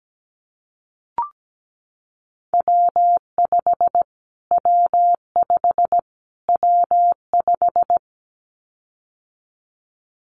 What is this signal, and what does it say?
Morse code, heard as a keyed tone, about 17 wpm, 700 Hz. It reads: W5 W5 W5
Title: Callsign Prefixes for Top 25 DXCC - Repeated 3x - 17wpm